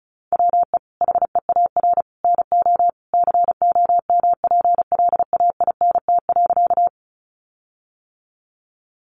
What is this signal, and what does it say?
Morse code, heard as a keyed tone, about 34 wpm, 700 Hz. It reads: WE HEAR NO COMPLAINT.
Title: Sentences from Top 1400 Words 35wpm